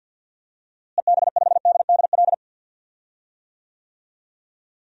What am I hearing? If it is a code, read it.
Morse code, heard as a keyed tone, about 47 wpm, 700 Hz. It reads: EB5DDL
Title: DXCC-100 Call Signs - 6 Characters - Repeated 2x - 50wpm